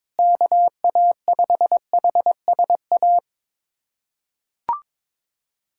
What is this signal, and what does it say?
Morse code, heard as a keyed tone, about 22 wpm, 700 Hz. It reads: KA5HSA
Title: USA Callsign DE Callsign - 6 Characters - 22wpm